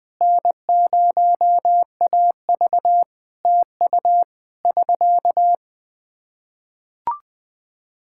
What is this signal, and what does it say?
Morse code, heard as a keyed tone, about 20 wpm, 700 Hz. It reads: N0AV TU <SK>
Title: K1USN SST QSO Practice 20wpm